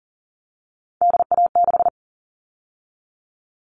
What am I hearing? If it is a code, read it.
Morse code, heard as a keyed tone, about 40 wpm, 700 Hz. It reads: DA6
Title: All Cut Numbers - 3-Digit 40wpm